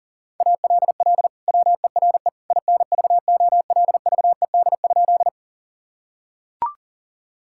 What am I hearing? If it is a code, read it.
Morse code, heard as a keyed tone, about 40 wpm, 700 Hz. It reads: ALL WERE INVOLVED?
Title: Sentences from Top 1000 Words - Review Part 5 - 40wpm